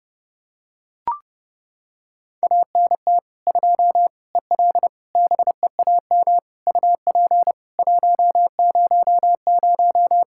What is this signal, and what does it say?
Morse code, heard as a keyed tone, about 30 wpm, 700 Hz. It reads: ANT 2 EL BEAM UP 100
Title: Practice QSO - Exchange 2 - Info - 30wpm